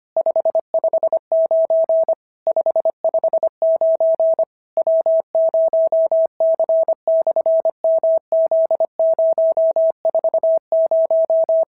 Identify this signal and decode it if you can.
Morse code, heard as a keyed tone, about 23 wpm, 650 Hz. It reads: 559 559 W0C/MZ040
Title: SOTA QSO Practice 25wpm